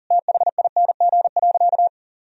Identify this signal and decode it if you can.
Morse code, heard as a keyed tone, about 40 wpm, 700 Hz. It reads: THING.
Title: Sentences from Top 1000 Words - Review Part 1 - 40wpm